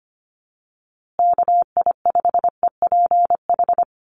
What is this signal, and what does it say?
Morse code, heard as a keyed tone, about 25 wpm, 700 Hz. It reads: KI5EPH
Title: USA Call Signs Repeated 2x 25wpm